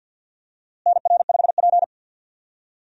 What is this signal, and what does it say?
Morse code, heard as a keyed tone, about 50 wpm, 700 Hz. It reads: NR5P
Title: USA Call Signs - 4 Characters - Repeated 3x - 50wpm